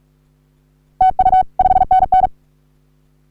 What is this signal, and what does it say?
Morse code, heard as a keyed tone, about 38 wpm, 750 Hz. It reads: TU 5NN
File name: CW / HA3OU RHOBCW REC